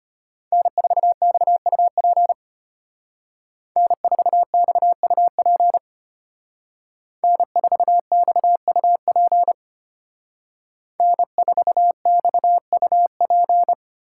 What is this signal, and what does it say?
Morse code, heard as a keyed tone, about 33 wpm, 700 Hz. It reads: N4XUP N4XUP N4XUP N4XUP